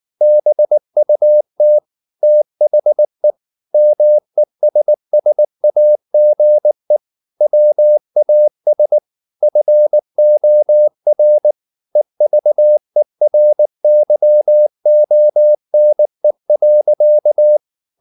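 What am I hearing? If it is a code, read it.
Morse code, heard as a keyed tone, about 19 wpm, 600 Hz. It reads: BUT THE MESSAGE WAS FOR EVERYONE.
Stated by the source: Morse code at 19 wpm from CW med Gnister, reading Ken Follett's novel Never in English.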